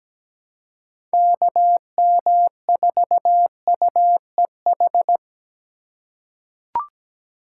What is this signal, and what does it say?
Morse code, heard as a keyed tone, about 17 wpm, 700 Hz. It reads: KM4UEH
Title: USA Call Signs - 6 Characters - 2x Character Spacing - 17wpm